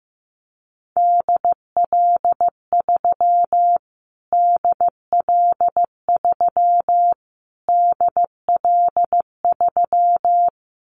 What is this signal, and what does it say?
Morse code, heard as a keyed tone, about 15 wpm, 700 Hz. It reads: DL3 DL3 DL3
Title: Callsign Prefixes for Top 25 DXCC - Repeated 3x - 15wpm